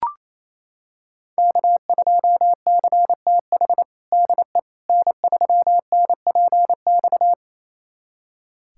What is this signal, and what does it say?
Morse code, heard as a keyed tone, about 28 wpm, 700 Hz. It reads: K2CTH DE N3NPX
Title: USA Callsign DE Callsign - 5 Characters - 28wpm